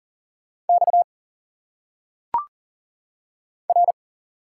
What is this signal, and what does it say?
Morse code, heard as a keyed tone, about 39 wpm, 700 Hz. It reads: X R
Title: Single Letters - Phonetic Pronunciation 40wpm